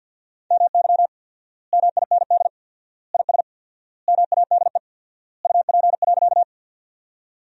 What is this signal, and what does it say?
Morse code, heard as a keyed tone, about 50 wpm, 700 Hz. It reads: MY MIND IS MADE UP.
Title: 4x Word-Spacing - Sentences from Top 500 Words - 50wpm